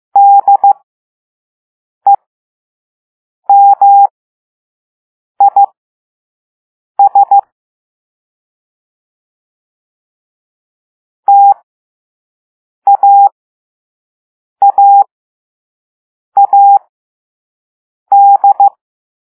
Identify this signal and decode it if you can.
Morse code, heard as a keyed tone, about 15 wpm, spaced as for 5 wpm, 800 Hz. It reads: DEMIS TAAAD